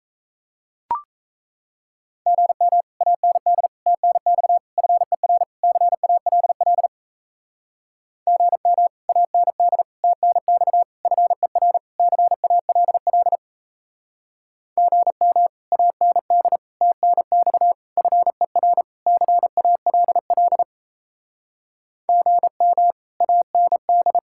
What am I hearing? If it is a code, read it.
Morse code, heard as a keyed tone, about 38 wpm, 700 Hz. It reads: GM AND TNX FER CALL GM AND TNX FER CALL GM AND TNX FER CALL GM AND